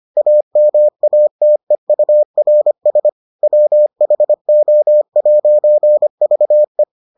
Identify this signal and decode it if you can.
Morse code, heard as a keyed tone, about 25 wpm, 600 Hz. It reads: AMATEURS WHO'VE